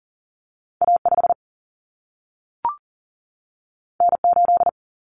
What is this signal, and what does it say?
Morse code, heard as a keyed tone, about 40 wpm, 700 Hz. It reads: A 5 N 8
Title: Common Cut Numbers - 2-Digit 40wpm